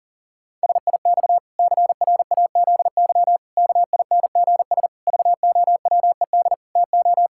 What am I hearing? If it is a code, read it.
Morse code, heard as a keyed tone, about 40 wpm, 700 Hz. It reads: SIX CRAZY KINGS VOWED TO
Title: Pangram Challenge v2 - 40wpm